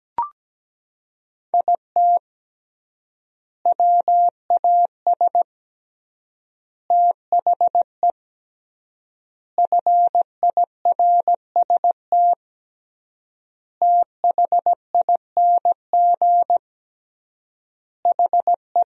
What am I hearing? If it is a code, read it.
Morse code, heard as a keyed tone, about 17 wpm, 700 Hz. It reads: IT WAS THE FIRST THING HE
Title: 3x Word-Spacing - Sentences from Top 300 Words - 17wpm